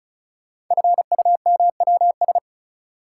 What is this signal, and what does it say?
Morse code, heard as a keyed tone, about 36 wpm, 700 Hz. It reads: FUMWS
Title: Code Groups - 5 Letters - 35wpm